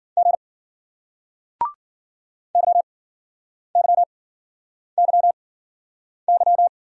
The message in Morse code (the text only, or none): G YYYY